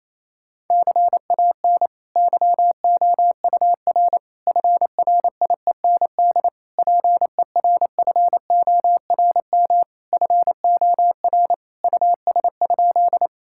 Morse code, heard as a keyed tone, about 28 wpm, 700 Hz. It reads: CAN YOUR FRIEND PERFORM FOR US?
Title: Sentences from Top 1000 Words - Review Part 5 - 28wpm